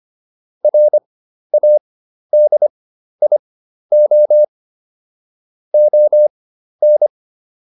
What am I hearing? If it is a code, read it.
Morse code, heard as a keyed tone, about 25 wpm, spaced as for 12 wpm, 600 Hz. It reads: RADIO ON